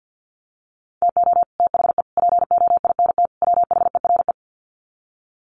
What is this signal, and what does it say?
Morse code, heard as a keyed tone, about 50 wpm, 700 Hz. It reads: TO THE POINT WHERE
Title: Sets of 4 Words 50wpm